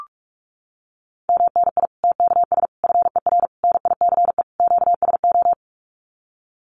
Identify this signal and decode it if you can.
Morse code, heard as a keyed tone, about 44 wpm, 700 Hz. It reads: MNI TKS FER NICE QSO
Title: Common QSO Phrases 45wpm